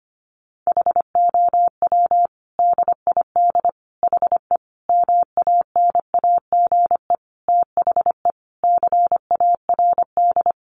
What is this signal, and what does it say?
Morse code, heard as a keyed tone, about 25 wpm, 700 Hz. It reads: HOW DID HE MANAGE THE CARD
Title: Sentences from Top 1500 Words 25wpm